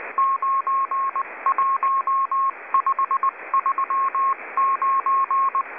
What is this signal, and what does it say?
Morse code, heard as a keyed tone, about 19 wpm, 1080 Hz. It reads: 91539